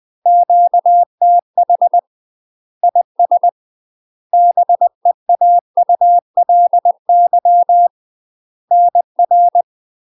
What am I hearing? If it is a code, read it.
Morse code, heard as a keyed tone, about 20 wpm, 700 Hz. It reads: QTH IS BEAULY NR
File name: UK MASTER QSOs TRAINING / 20WPM / UK MASTER QSOs 0023